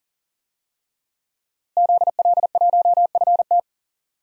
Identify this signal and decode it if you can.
Morse code, heard as a keyed tone, about 37 wpm, 700 Hz. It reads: ZL1FT
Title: DXCC-100 Call Signs - 5 Characters - Repeated 2x - 40wpm